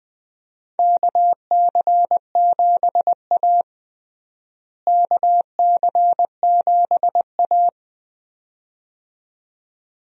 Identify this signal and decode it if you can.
Morse code, heard as a keyed tone, about 21 wpm, 700 Hz. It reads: KC7A KC7A
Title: USA Call Signs - 4 Characters - Repeated 2x - 20wpm